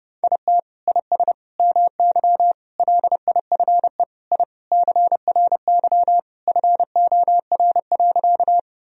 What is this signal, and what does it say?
Morse code, heard as a keyed tone, about 30 wpm, 700 Hz. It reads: IT IS MY LIFE I CRY FOR.
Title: Sentences from Top 1000 Words - Review Part 2 - 30wpm